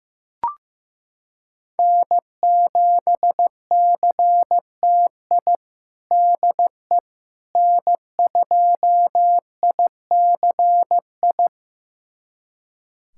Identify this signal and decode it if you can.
Morse code, heard as a keyed tone, about 15 wpm, 700 Hz. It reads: N7CTI DE N2ICI
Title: USA Callsign DE Callsign - 5 Characters - 15wpm